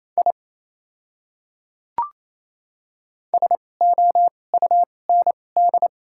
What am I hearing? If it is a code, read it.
Morse code, heard as a keyed tone, about 28 wpm, spaced as for 20 wpm, 700 Hz. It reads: I SOUND
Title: 2x Character Spacing - Top 100 Words - 28wpm